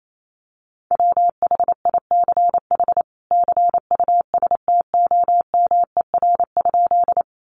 Morse code, heard as a keyed tone, about 28 wpm, 700 Hz. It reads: WHICH CUSTOMER?
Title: Sentences from Top 1100 Words 28wpm